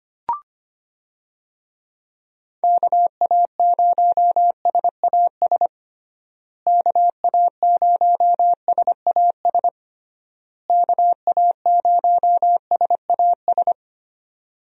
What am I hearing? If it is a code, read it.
Morse code, heard as a keyed tone, about 25 wpm, 700 Hz. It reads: KA0SAS KA0SAS KA0SAS
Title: USA Call Signs - 6 Characters - Repeated 3x - 25wpm